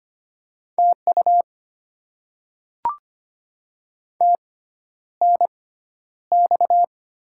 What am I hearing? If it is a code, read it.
Morse code, heard as a keyed tone, about 25 wpm, spaced as for 11 wpm, 700 Hz. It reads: TU TNX
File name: Top 20 QSO Elements - 6x Character Spacing - 25wpm